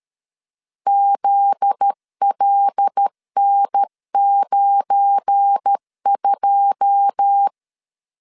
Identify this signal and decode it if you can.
Morse code, heard as a keyed tone, about 13 wpm, 800 Hz. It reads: ZLN92
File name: LEZIONI / 9 28 12wpm 2351